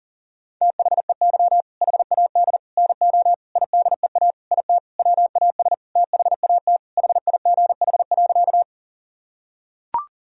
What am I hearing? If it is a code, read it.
Morse code, heard as a keyed tone, about 40 wpm, 700 Hz. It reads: THEY HAD NO IDEA IT WAS THAT HIGH.
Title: Sentences from Top 1000 Words - Review Part 1 - 40wpm